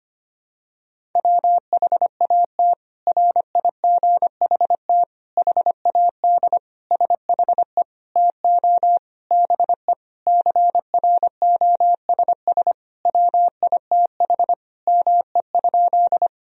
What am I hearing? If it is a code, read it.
Morse code, heard as a keyed tone, about 25 wpm, 700 Hz. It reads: WHAT RIGHT HAD SHE TO BE CROSS WITH ME?
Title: Sentences from Top 1000 Words - Review Part 1 - 25wpm